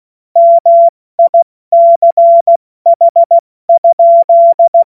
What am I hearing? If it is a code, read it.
Morse code, heard as a keyed tone, about 16 wpm, 680 Hz. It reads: MICH?
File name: Morsekurs / Witze / Witz 06. Juli 2021, 70 F80